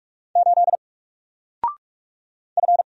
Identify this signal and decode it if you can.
Morse code, heard as a keyed tone, about 44 wpm, 700 Hz. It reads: 8 F
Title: Single Character (Letter:Number:Punctuation:Prosign) Alt 45wpm